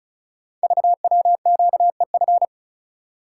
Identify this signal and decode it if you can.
Morse code, heard as a keyed tone, about 35 wpm, 690 Hz. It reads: VWQEF
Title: Code Groups - 5 Letters - 35wpm